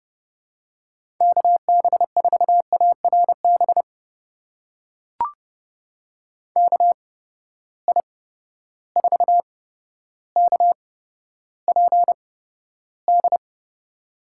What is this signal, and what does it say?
Morse code, heard as a keyed tone, about 30 wpm, spaced as for 13 wpm, 700 Hz. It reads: KB4ARB KI4KPD
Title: USA Call Signs - 8x Character Spacing - 30wpm